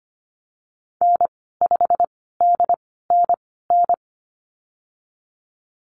Morse code, heard as a keyed tone, about 25 wpm, spaced as for 16 wpm, 700 Hz. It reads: N5DNN
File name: Lesson 15g - Number 5 - Call Signs 25wpm